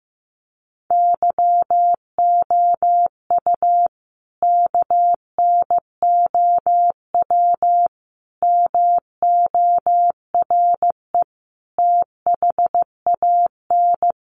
Morse code, heard as a keyed tone, about 15 wpm, 700 Hz. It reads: YOU KNOW MORE THAN
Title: Sentences from Top 100 Words 15wpm